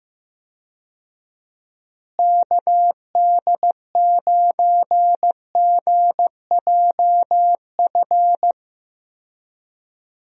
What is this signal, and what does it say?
Morse code, heard as a keyed tone, about 16 wpm, 700 Hz. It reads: KD9GJF
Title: USA Call Signs - 6 Characters - Repeated 2x - 15wpm